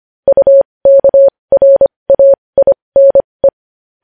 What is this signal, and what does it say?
Morse code, heard as a keyed tone, about 25 wpm, spaced as for 20 wpm, 550 Hz. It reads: UKRAINE
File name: podcast / News headlines-1x-25x20